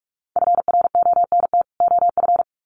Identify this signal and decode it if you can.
Morse code, heard as a keyed tone, about 44 wpm, 700 Hz. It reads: FRONT OF
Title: Sets of 2 Words 45wpm